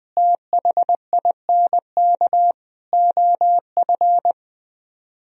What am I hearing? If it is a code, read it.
Morse code, heard as a keyed tone, about 21 wpm, 700 Hz. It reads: THINK OF